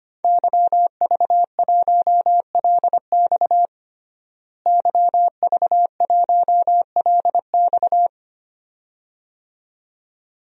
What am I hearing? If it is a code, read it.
Morse code, heard as a keyed tone, about 25 wpm, 700 Hz. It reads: YV1LX YV1LX
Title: DXCC-100 Call Signs - 5 Characters - Repeated 2x - 25wpm